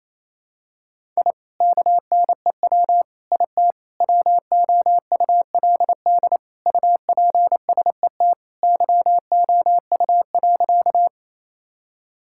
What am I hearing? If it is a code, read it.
Morse code, heard as a keyed tone, about 27 wpm, 700 Hz. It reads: I KNEW IT WOULD UPSET YOU.
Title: Sentences from Top 1000 Words - Review Part 5 - 28wpm